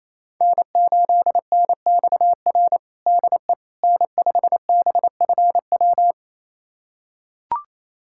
Morse code, heard as a keyed tone, about 28 wpm, 700 Hz. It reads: N8NXR DE N5BFW
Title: USA Callsign DE Callsign - 5 Characters - 28wpm